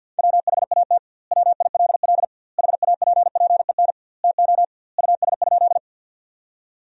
Morse code, heard as a keyed tone, about 50 wpm, 700 Hz. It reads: WHAT WILL HAPPEN TO US?
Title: 2x Word-Spacing - Sentences from Top 400 Words - 50wpm